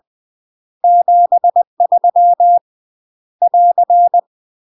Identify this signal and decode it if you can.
Morse code, heard as a keyed tone, about 20 wpm, 700 Hz. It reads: 73 <AR>
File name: UK MASTER QSOs TRAINING / 20WPM / UK MASTER QSOs 0061